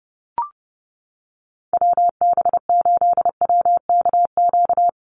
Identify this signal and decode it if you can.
Morse code, heard as a keyed tone, about 30 wpm, 700 Hz. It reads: WB8WKQ
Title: Call Signs - 30wpm